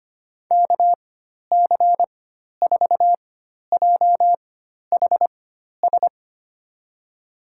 Morse code, heard as a keyed tone, about 25 wpm, spaced as for 12 wpm, 700 Hz. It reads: KC4JHS
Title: USA Call Signs - 4x Character Spacing - 25wpm